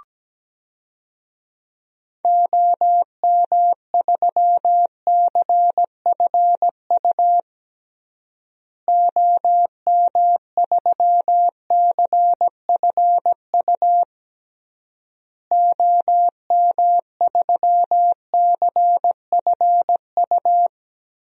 Morse code, heard as a keyed tone, about 17 wpm, 700 Hz. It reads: OM3CFU OM3CFU OM3CFU
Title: DXCC-100 Call Signs - 6 Characters - Repeated 3x - 17wpm